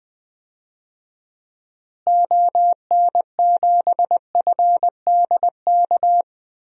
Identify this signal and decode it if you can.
Morse code, heard as a keyed tone, about 20 wpm, 700 Hz. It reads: ON7FDK
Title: DXCC-100 Call Signs - 6 Characters - Repeated 3x - 20wpm